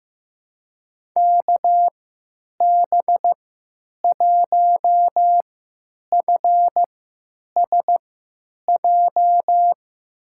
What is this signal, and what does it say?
Morse code, heard as a keyed tone, about 15 wpm, spaced as for 9 wpm, 700 Hz. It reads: KB1FSJ
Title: USA Call Signs - 3x Character Spacing - 15wpm